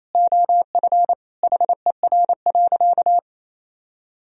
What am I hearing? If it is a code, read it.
Morse code, heard as a keyed tone, about 28 wpm, 700 Hz. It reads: OF HER.
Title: Sentences from Top 1000 Words - Review Part 2 - 28wpm